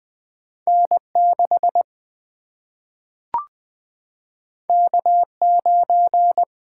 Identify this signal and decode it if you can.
Morse code, heard as a keyed tone, about 20 wpm, 700 Hz. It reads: N 6 K 9
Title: Single-Letter Single-Number 20wpm